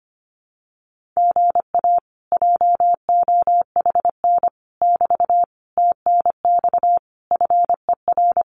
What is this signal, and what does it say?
Morse code, heard as a keyed tone, about 25 wpm, 700 Hz. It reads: GA JOHN <BT> TNX FER
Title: Common QSO Phrases Encore 25wpm